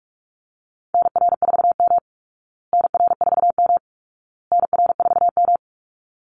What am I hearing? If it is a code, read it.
Morse code, heard as a keyed tone, about 44 wpm, 700 Hz. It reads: NR4M NR4M NR4M
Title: Call Signs Repeated 2x - 45wpm